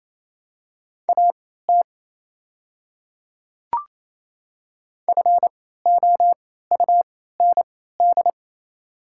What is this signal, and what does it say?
Morse code, heard as a keyed tone, about 28 wpm, spaced as for 16 wpm, 700 Hz. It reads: AT FOUND
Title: 3x Character Spacing - Top 300 Words - 28wpm